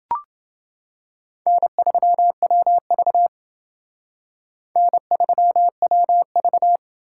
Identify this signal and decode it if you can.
Morse code, heard as a keyed tone, about 28 wpm, 700 Hz. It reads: N3WV N3WV